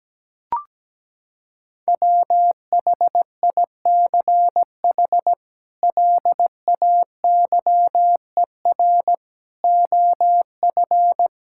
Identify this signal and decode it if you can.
Morse code, heard as a keyed tone, about 17 wpm, 700 Hz. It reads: WHICH LAYER OF
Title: Sentences from Top 1600 Words 17wpm